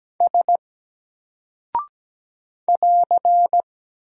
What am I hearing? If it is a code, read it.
Morse code, heard as a keyed tone, about 17 wpm, 700 Hz. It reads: S <AR>
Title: Single Character (Letter:Number:Punctuation:Prosign) 17wpm